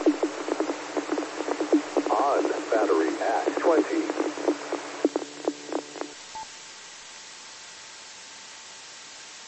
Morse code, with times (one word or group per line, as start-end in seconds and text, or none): none